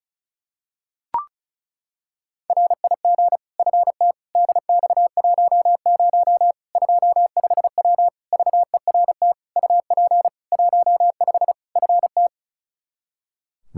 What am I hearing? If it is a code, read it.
Morse code, heard as a keyed tone, about 36 wpm, 700 Hz. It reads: RIG FT DX10 25W VERT UP 15 FT
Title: K1USN SST QSO Practice 35wpm